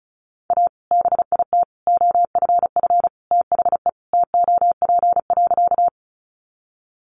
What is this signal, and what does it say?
Morse code, heard as a keyed tone, about 34 wpm, 700 Hz. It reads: A BIT OFF THE TOP.
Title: Sentences from Top 900 Words 35wpm